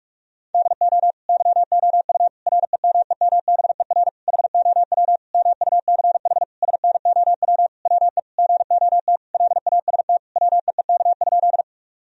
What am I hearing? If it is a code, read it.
Morse code, heard as a keyed tone, about 46 wpm, 700 Hz. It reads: DO YOU REMEMBER HOW MUCH SNOW WE GOT LAST WEEK?
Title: Sentences from Top 1000 Words - Review Part 3 - 45wpm